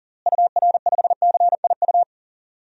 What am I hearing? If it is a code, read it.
Morse code, heard as a keyed tone, about 40 wpm, 700 Hz. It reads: UR5CIU
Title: DXCC-100 Call Signs - 6 Characters - Repeated 3x - 40wpm